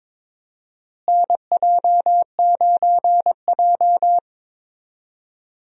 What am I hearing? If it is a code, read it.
Morse code, heard as a keyed tone, about 22 wpm, 700 Hz. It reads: NJ9J
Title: USA Call Signs - 4 Characters - Repeated 2x - 22wpm